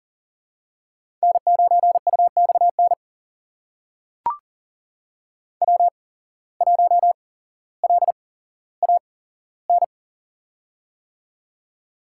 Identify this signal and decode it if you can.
Morse code, heard as a keyed tone, about 40 wpm, spaced as for 17 wpm, 700 Hz. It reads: N9UXN W1LAN
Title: USA Call Signs - 5 Characters - 8x Character Spacing - 40wpm